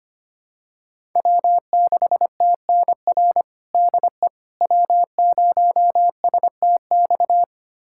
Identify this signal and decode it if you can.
Morse code, heard as a keyed tone, about 25 wpm, 700 Hz. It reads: W6TNR DE W0STX
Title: USA Callsign DE Callsign - 5 Characters - 25wpm